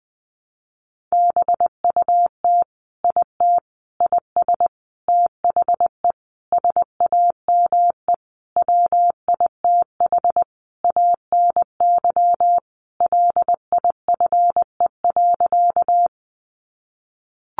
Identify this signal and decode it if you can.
Morse code, heard as a keyed tone, about 20 wpm, 700 Hz. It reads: BUT IT IS THE SAME WITH ANY LIFE.
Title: Sentences from Top 300 Words 20wpm